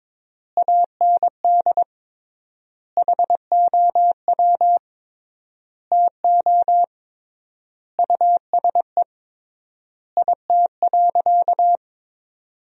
Morse code, heard as a keyed tone, about 21 wpm, 700 Hz. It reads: AND HOW TO USE IT.
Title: 3x Word-Spacing - Sentences from Top 500 Words - 22wpm